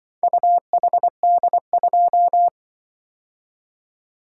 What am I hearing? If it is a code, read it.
Morse code, heard as a keyed tone, about 24 wpm, 700 Hz. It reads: UHD2
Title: Speed-Racing - Four Letter-Number Combinations - 20wpm